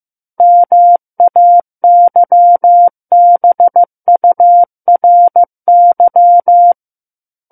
Morse code, heard as a keyed tone, about 15 wpm, 700 Hz. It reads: MAYBURY
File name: Book One / 06-Chapter 6 - 15 WPM